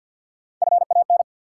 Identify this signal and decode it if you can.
Morse code, heard as a keyed tone, about 53 wpm, 700 Hz. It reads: FAN
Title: Sentences from Top 1000 Words - Review Part 4 - New Word Drill - 50wpm